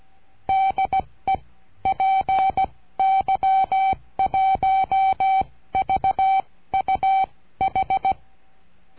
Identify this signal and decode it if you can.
Morse code, heard as a keyed tone, about 16 wpm, spaced as for 14 wpm, 770 Hz. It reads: DE PY1VUH
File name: cursos / CURSO DE TELEGRAFIA / Telegrafia - Trilha 22